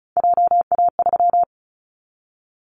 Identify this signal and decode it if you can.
Morse code, heard as a keyed tone, about 36 wpm, 700 Hz. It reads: JA3
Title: Callsign Prefixes for Top 25 DXCC - 35wpm